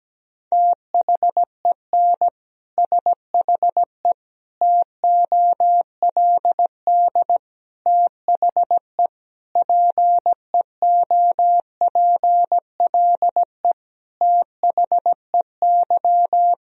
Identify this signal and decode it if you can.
Morse code, heard as a keyed tone, about 17 wpm, 700 Hz. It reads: THEN SHE TOLD THE PEOPLE THEY
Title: Sentences from Top 1000 Words - Review Part 2 - 17wpm